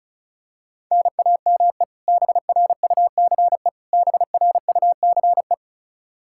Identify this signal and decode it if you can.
Morse code, heard as a keyed tone, about 36 wpm, 700 Hz. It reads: NAME BRUCE BRUCE
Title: Practice QSO - Exchange 1 - Beginning - 35wpm